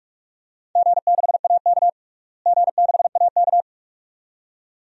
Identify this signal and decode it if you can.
Morse code, heard as a keyed tone, about 46 wpm, 700 Hz. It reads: G6AK G6AK
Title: DXCC-100 Call Signs - 4 Characters - Repeated 2x - 45wpm